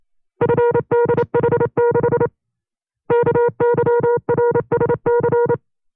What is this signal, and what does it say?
Morse code, heard as a keyed tone, about 28 wpm, 480 Hz. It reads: FDH6 KYRSC
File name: FreeFiles / CW-practice / 25WPM-side1